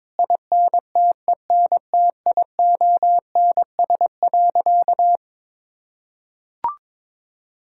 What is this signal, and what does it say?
Morse code, heard as a keyed tone, about 22 wpm, 700 Hz. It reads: INTENTIONS.
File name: practice / 22wpm / Sentences with 8-Word Phrases - 22wpm